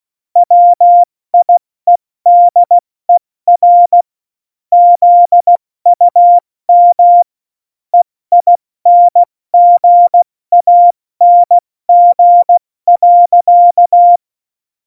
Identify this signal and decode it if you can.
Morse code, heard as a keyed tone, about 16 wpm, 700 Hz. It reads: WIEDER ZUM EINGANG.
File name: Morsekurs